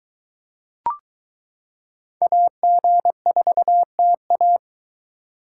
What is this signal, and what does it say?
Morse code, heard as a keyed tone, about 22 wpm, 700 Hz. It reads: AG4TA